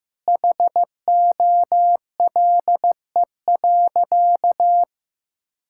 Morse code, heard as a keyed tone, about 15 wpm, 700 Hz. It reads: HOLE.
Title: Sentences from Top 1600 Words 15wpm